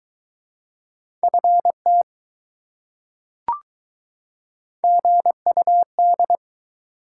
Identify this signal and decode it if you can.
Morse code, heard as a keyed tone, about 21 wpm, 700 Hz. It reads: FT GUD